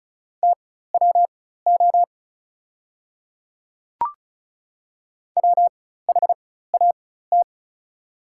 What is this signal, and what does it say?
Morse code, heard as a keyed tone, about 34 wpm, spaced as for 16 wpm, 690 Hz. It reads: TWO WHAT
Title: 4x Character Spacing - Top 100 Words - 35wpm